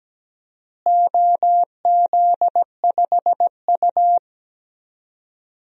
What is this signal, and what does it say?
Morse code, heard as a keyed tone, about 17 wpm, 700 Hz. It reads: OZ5U
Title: DXCC-100 Call Signs - 4 Characters - Repeated 3x - 17wpm